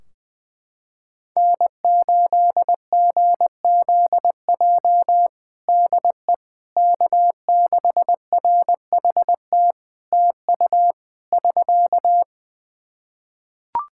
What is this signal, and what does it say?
Morse code, heard as a keyed tone, about 20 wpm, 700 Hz. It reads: N8GZJ DE K6RHT TU <SK>
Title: Practice QSO - Exchange 4 - Ending - 20wpm